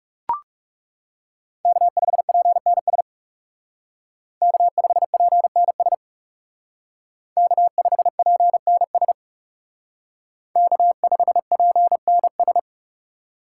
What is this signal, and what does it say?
Morse code, heard as a keyed tone, about 36 wpm, 700 Hz. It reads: K5PNS K5PNS K5PNS K5PNS
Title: Speed Racing - USA 1x3 Call Signs 30wpm